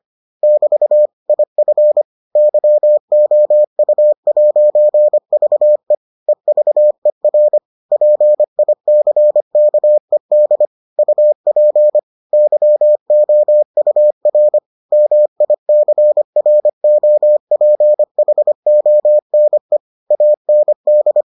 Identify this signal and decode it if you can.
Morse code, heard as a keyed tone, about 25 wpm, 600 Hz. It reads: <BT> IF YOU'VE EVER PICKED UP YOUR MICROPHONE AND